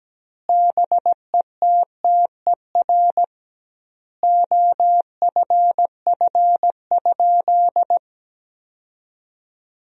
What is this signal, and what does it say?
Morse code, heard as a keyed tone, about 17 wpm, 700 Hz. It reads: BETTER OFF?
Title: 2x Word-Spacing - Sentences from Top 500 Words - 17wpm